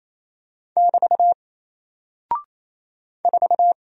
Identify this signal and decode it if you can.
Morse code, heard as a keyed tone, about 28 wpm, 700 Hz. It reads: <BT> 4
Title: Single Character (Letter:Number:Punctuation:Prosign) 28wpm